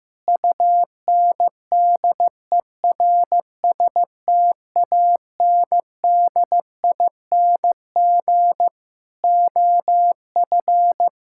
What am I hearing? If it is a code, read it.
Morse code, heard as a keyed tone, about 15 wpm, 700 Hz. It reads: UNDERSTANDING OF
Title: Sentences from Top 1000 Words - Review Part 6 - 15wpm